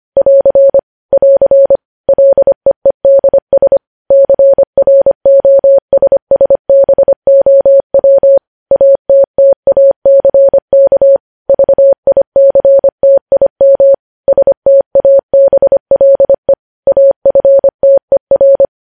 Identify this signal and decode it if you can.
Morse code, heard as a keyed tone, about 25 wpm, 550 Hz. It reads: <AR> <AR> LEEDS CROSSBOW ATTACK VICTIM STABLE AFTER